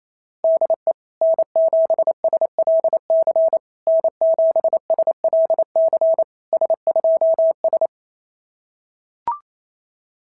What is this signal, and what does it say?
Morse code, heard as a keyed tone, about 25 wpm, 650 Hz. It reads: DE N7SLC N7SLC S2S
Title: SOTA QSO Practice 28wpm